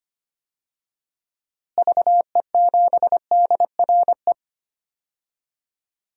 Morse code, heard as a keyed tone, about 25 wpm, 700 Hz. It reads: VE7DRE
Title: DXCC-100 Call Signs - 6 Characters - Repeated 2x - 25wpm